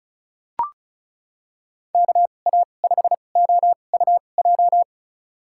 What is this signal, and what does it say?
Morse code, heard as a keyed tone, about 36 wpm, spaced as for 25 wpm, 700 Hz. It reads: KA5OUJ
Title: USA Call Signs - 6 Characters - 2x Character Spacing - 35wpm